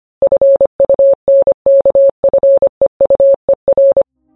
Morse code, heard as a keyed tone, about 25 wpm, 550 Hz. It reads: FUNKFEUER
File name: funkfeuer / funkfeuer-files / Q-und-Zahlencodes-Teil-2